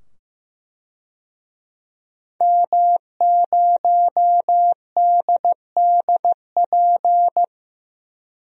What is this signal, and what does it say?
Morse code, heard as a keyed tone, about 15 wpm, 700 Hz. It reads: M0DDP
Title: DXCC-100 Call Signs - 5 Characters - 15wpm